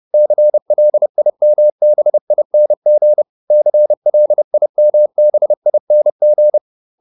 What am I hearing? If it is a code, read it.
Morse code, heard as a keyed tone, about 30 wpm, 600 Hz. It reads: CLIMBING CLIMBING
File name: ING8 30